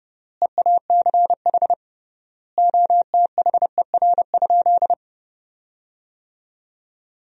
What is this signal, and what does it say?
Morse code, heard as a keyed tone, about 29 wpm, 700 Hz. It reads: EACH OTHER?
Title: 3x Word-Spacing - Sentences from Top 200 Words - 30wpm